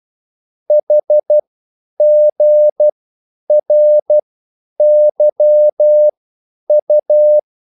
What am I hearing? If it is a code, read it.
Morse code, heard as a keyed tone, about 12 wpm, spaced as for 9 wpm, 600 Hz. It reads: HGRYU
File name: QSL / CW lesson